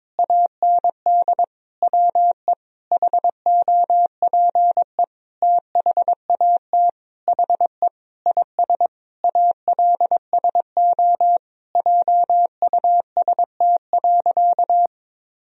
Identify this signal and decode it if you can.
Morse code, heard as a keyed tone, about 22 wpm, 700 Hz. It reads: AND WE HOPE THAT HE IS ALSO JUST.
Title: Sentences from Top 1000 Words - Review Part 1 - 22wpm